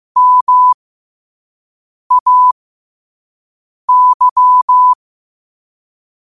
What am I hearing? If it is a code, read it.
Morse code, heard as a keyed tone, about 15 wpm, spaced as for 5 wpm, 1000 Hz. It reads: MAY
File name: qotdfiles / 5wpm